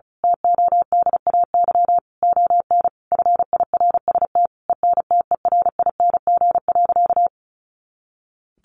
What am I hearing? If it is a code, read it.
Morse code, heard as a keyed tone, about 34 wpm, 700 Hz. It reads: TODAY ON FIRST ENTERING.
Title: A Sound Of Thunder 35wpm